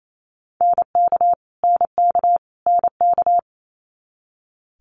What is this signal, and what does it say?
Morse code, heard as a keyed tone, about 28 wpm, 700 Hz. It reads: NK NK NK
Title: Multi-letter Phonemes Repeated 3x 28wpm